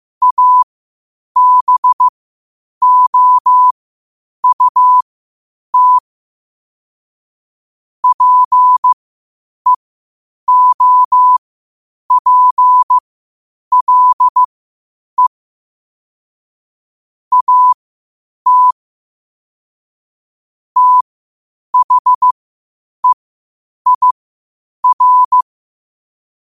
Morse code, heard as a keyed tone, about 15 wpm, spaced as for 8 wpm, 1000 Hz. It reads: ABOUT PEOPLE AT THEIR
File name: qotdfiles / 8wpm